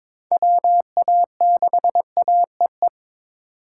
Morse code, heard as a keyed tone, about 22 wpm, 700 Hz. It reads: WA6AEE
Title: K1USN SST QSO Practice - 22wpm